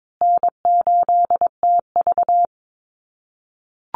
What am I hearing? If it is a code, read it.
Morse code, heard as a keyed tone, about 22 wpm, 700 Hz. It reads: N8TV
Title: USA Call Signs Repeated 3x 22wpm